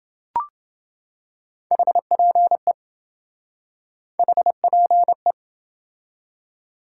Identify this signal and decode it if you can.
Morse code, heard as a keyed tone, about 27 wpm, 700 Hz. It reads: HPE HPE